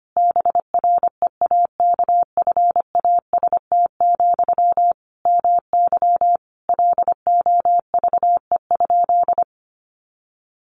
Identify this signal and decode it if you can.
Morse code, heard as a keyed tone, about 25 wpm, 700 Hz. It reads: BREAKFAST, MY LOVE?
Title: Sentences from Top 1500 Words 25wpm